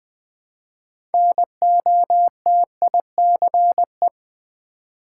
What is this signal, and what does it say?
Morse code, heard as a keyed tone, about 20 wpm, 700 Hz. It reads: NOTICE
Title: 6 Letter Words - First 3 Letters Given - 20wpm